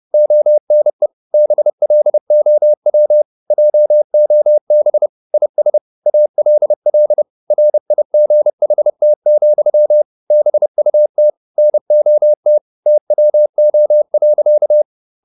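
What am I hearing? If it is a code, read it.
Morse code, heard as a keyed tone, about 30 wpm, 600 Hz. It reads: ONE BLOW JOB IS ALL RIGHT, BUT NOT TWO.
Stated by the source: Morse code at 30 wpm, the novel Never by Ken Follett in English, from CW med Gnister.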